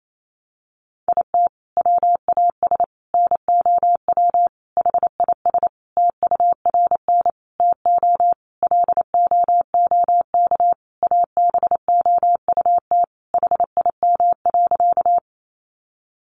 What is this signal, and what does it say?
Morse code, heard as a keyed tone, about 28 wpm, 700 Hz. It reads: IT WAS NOW HIS TURN TO LOOK ABOUT HIM.
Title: Sentences from Top 200 Words 28wpm